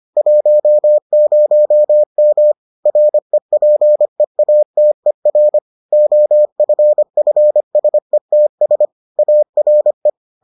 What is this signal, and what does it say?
Morse code, heard as a keyed tone, about 25 wpm, 600 Hz. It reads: 10M REPEATER OFFSETS ARE